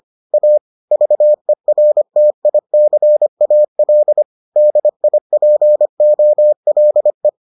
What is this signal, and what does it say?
Morse code, heard as a keyed tone, about 25 wpm, 600 Hz. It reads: A VERTICAL DIPOLE